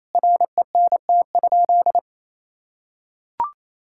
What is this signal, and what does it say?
Morse code, heard as a keyed tone, about 28 wpm, 700 Hz. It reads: RENT?
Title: Sentences from Top 1000 Words - Review Part 5 - 28wpm